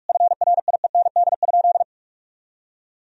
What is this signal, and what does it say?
Morse code, heard as a keyed tone, about 43 wpm, 700 Hz. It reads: FRIEND?
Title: Sentences from Top 1000 Words - Review Part 3 - 45wpm